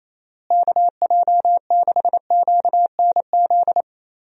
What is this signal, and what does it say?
Morse code, heard as a keyed tone, about 27 wpm, 700 Hz. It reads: KJ6QNZ
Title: USA Call Signs - 6 Characters - Repeated 2x - 28wpm